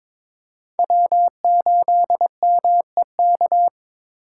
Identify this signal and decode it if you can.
Morse code, heard as a keyed tone, about 26 wpm, 700 Hz. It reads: W8MEK
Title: Speed Racing - USA 1x3 Call Signs 22wpm